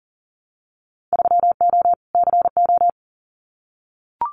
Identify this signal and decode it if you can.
Morse code, heard as a keyed tone, about 42 wpm, 700 Hz. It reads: 3O CO
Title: Winter Field Day Exchanges Encore 40wpm